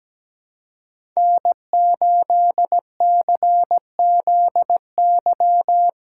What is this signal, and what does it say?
Morse code, heard as a keyed tone, about 17 wpm, 700 Hz. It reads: N8CZY
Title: USA Callsign DE Callsign - 5 Characters - 17wpm